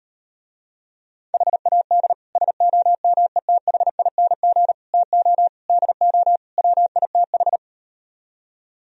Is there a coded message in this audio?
Yes: Morse code from 1.3 to 7.6 s, heard as a keyed tone, about 38 wpm, 700 Hz. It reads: HAD SOMETHING TO DO WITH